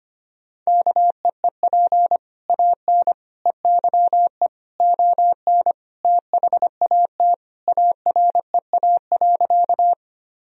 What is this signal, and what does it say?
Morse code, heard as a keyed tone, about 25 wpm, 700 Hz. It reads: KEEP AN EYE ON THAT AREA.
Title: Sentences from Top 1000 Words - Review Part 2 - 25wpm